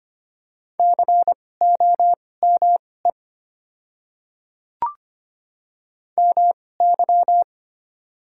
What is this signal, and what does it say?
Morse code, heard as a keyed tone, about 25 wpm, spaced as for 18 wpm, 700 Hz. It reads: COME MY
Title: 2x Character Spacing - Top 100 Words - 25wpm